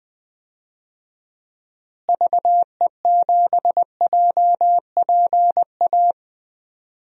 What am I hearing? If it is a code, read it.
Morse code, heard as a keyed tone, about 21 wpm, 700 Hz. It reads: VE7JPA
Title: DXCC-100 Call Signs - 6 Characters - Repeated 2x - 20wpm